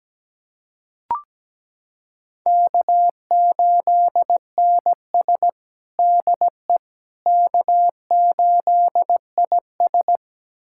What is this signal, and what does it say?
Morse code, heard as a keyed tone, about 17 wpm, 700 Hz. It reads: K8NS DE K8IS
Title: USA Callsign DE Callsign - 4 Characters - 17wpm